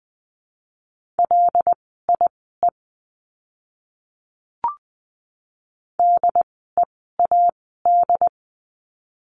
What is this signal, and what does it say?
Morse code, heard as a keyed tone, about 20 wpm, spaced as for 15 wpm, 700 Hz. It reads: LIE DEAD
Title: 075 Lesson 13f - Letter L - Words 20wpm